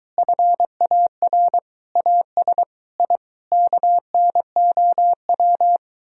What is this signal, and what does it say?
Morse code, heard as a keyed tone, about 23 wpm, 700 Hz. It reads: FAR AS I KNOW